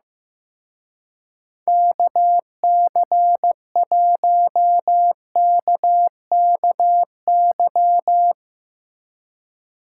Morse code, heard as a keyed tone, about 17 wpm, 700 Hz. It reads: KC1KKY
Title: Speed Racing - USA 2x3 Call Signs 15wpm